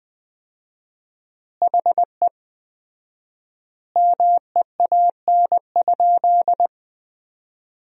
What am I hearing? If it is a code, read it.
Morse code, heard as a keyed tone, about 20 wpm, 700 Hz. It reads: HE MEAN?
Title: 4x Word-Spacing - Sentences from Top 200 Words - 20wpm